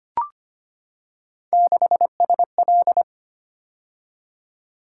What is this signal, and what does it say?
Morse code, heard as a keyed tone, about 25 wpm, 700 Hz. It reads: 6SL
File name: Three Letter-Number Combinations - 25wpm